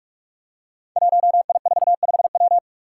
Morse code, heard as a keyed tone, about 42 wpm, 700 Hz. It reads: 1I45W
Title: Five Challenging Letter-Number Combinations 45wpm